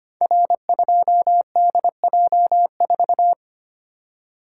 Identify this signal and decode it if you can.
Morse code, heard as a keyed tone, about 25 wpm, 700 Hz. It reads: R2DJ4
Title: Code Groups - 5 Letter:Numbers - 25wpm